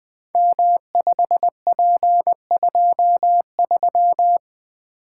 Morse code, heard as a keyed tone, about 21 wpm, 700 Hz. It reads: M5P23
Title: Code Groups - 5 Letter:Numbers - 20wpm